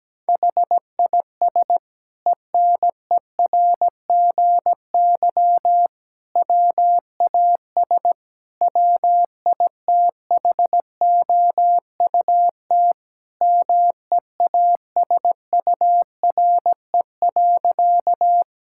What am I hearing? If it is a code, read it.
Morse code, heard as a keyed tone, about 17 wpm, 700 Hz. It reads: HIS ENERGY WAS WITHOUT MEASURE.
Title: Sentences from Top 1000 Words - Review Part 6 - 17wpm